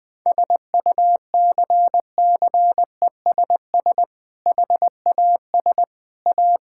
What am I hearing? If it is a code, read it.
Morse code, heard as a keyed tone, about 20 wpm, 700 Hz. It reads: SUCCESS HAS A
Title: Sentences with 6-Word Phrases - Spoken Beginning - 20wpm